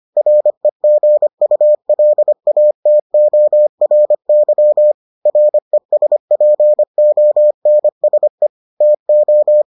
Morse code, heard as a keyed tone, about 25 wpm, 600 Hz. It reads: REGULATORY RESPONSE TO